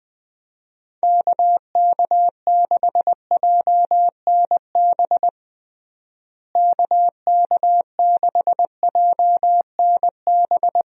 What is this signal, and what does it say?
Morse code, heard as a keyed tone, about 20 wpm, 700 Hz. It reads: KK6JNB KK6JNB
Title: USA Call Signs - 6 Characters - Repeated 3x - 20wpm